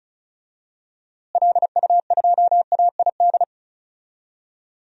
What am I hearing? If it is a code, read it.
Morse code, heard as a keyed tone, about 36 wpm, 700 Hz. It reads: LU2AID
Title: DXCC-100 Call Signs - 6 Characters - Repeated 3x - 35wpm